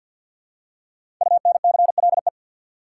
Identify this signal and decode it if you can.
Morse code, heard as a keyed tone, about 47 wpm, 700 Hz. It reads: UNCLE
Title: Five Letter Words - 50wpm